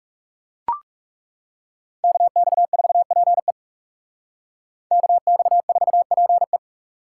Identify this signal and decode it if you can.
Morse code, heard as a keyed tone, about 34 wpm, 700 Hz. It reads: KX4PE KX4PE
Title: Speed Racing - USA Call Signs 30wpm